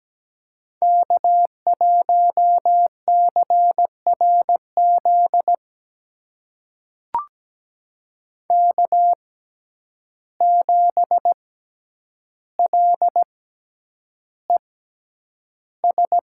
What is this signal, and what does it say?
Morse code, heard as a keyed tone, about 17 wpm, spaced as for 9 wpm, 700 Hz. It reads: K1CRZ K7LES
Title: USA Call Signs - 6x Character Spacing - 17wpm